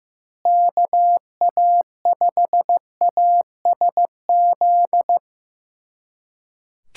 Morse code, heard as a keyed tone, about 15 wpm, 700 Hz. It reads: KA5ASZ
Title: USA Callsign DE Callsign - 6 Characters - 15wpm